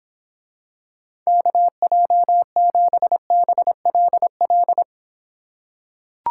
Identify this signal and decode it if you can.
Morse code, heard as a keyed tone, about 25 wpm, 700 Hz. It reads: KJ7BLL